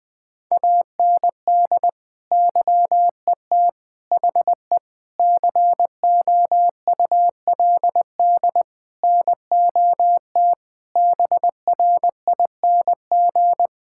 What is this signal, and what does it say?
Morse code, heard as a keyed tone, about 20 wpm, 700 Hz. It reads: AND YET HE COULD NOT BRING
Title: Sentences from Top 1600 Words 20wpm